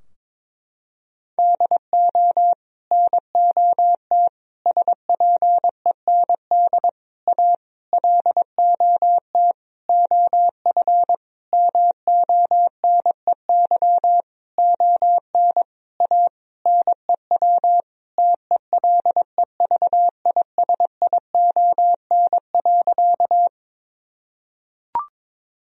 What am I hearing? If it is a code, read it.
Morse code, heard as a keyed tone, about 22 wpm, 700 Hz. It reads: DO NOT SPEND A LOT OF MONEY ON A NEW TELEVISION.
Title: Sentences from Top 1000 Words - Review Part 4 - 22wpm